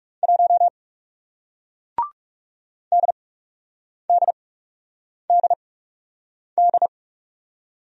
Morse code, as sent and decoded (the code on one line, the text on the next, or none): .---- / -.. -.. -.. -..
1 DDDD